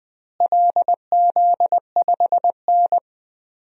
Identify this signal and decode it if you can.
Morse code, heard as a keyed tone, about 20 wpm, 700 Hz. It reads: LZ5N
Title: DXCC-100 Call Signs - 4 Characters - Repeated 2x - 20wpm